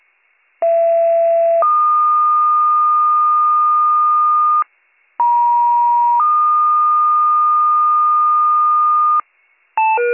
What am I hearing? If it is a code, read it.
Two-tone page, heard as a one-second tone, then a three-second tone, about 670 then 1220 Hz, about 950 then 1220 Hz. It is heard from 0.6 to 9.2 s.